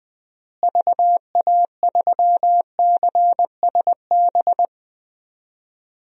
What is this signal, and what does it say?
Morse code, heard as a keyed tone, about 20 wpm, 700 Hz. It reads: VA3CSB
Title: DXCC-100 Call Signs - 6 Characters - Repeated 3x - 20wpm